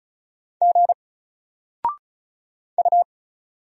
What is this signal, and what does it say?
Morse code, heard as a keyed tone, about 36 wpm, 700 Hz. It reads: G U
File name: Single-Letter-Number-35wpm